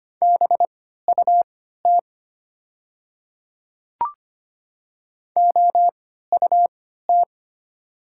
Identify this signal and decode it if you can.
Morse code, heard as a keyed tone, about 25 wpm, spaced as for 14 wpm, 690 Hz. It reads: BUT OUT
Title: 3x Character Spacing - Top 500 Words - 25wpm